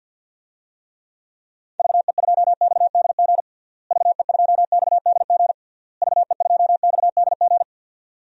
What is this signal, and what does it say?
Morse code, heard as a keyed tone, about 50 wpm, 700 Hz. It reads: VE2XDG VE2XDG VE2XDG
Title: DXCC-100 Call Signs - 6 Characters - Repeated 3x - 50wpm